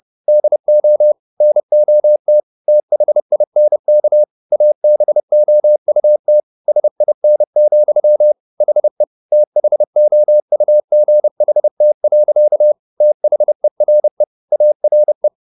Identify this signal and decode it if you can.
Morse code, heard as a keyed tone, about 30 wpm, 600 Hz. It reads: DO NOT THINK ABOUT SIN, HE THOUGHT. THERE ARE